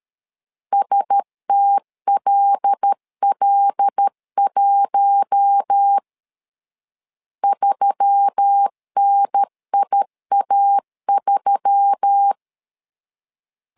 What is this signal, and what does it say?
Morse code, heard as a keyed tone, about 13 wpm, 800 Hz. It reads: STLL1 3NIA3
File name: LEZIONI / 5 LBVC / 5 L 12wpm 1913